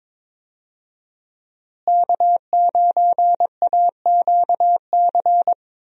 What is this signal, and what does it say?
Morse code, heard as a keyed tone, about 22 wpm, 700 Hz. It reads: K9AQC
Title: USA Call Signs - 5 Characters - Repeated 2x - 22wpm